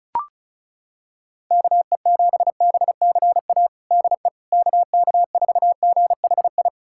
Morse code, heard as a keyed tone, about 36 wpm, 700 Hz. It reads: KE7BCA DE KK4GHI
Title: USA Callsign DE Callsign 35wpm